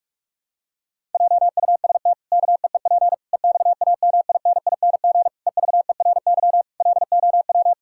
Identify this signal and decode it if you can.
Morse code, heard as a keyed tone, about 46 wpm, 700 Hz. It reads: JUST KEEP EXAMINING EVERY LOW
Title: Pangram Challenge v2 - 45wpm